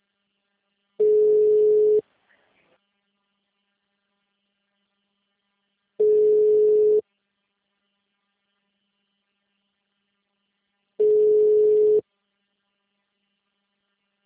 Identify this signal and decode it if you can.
Ringback tone (Europe, 425 Hz).
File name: out- -2017- - - .151